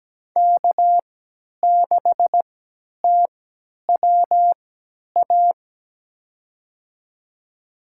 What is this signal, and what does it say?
Morse code, heard as a keyed tone, about 17 wpm, spaced as for 10 wpm, 700 Hz. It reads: K6TWA
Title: USA Call Signs - 5 Characters - 3x Character Spacing - 17wpm